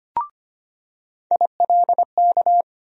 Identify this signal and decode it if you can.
Morse code, heard as a keyed tone, about 25 wpm, 700 Hz. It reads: ILK
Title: USA Call Sign Postfixes - 3 Letters 25wpm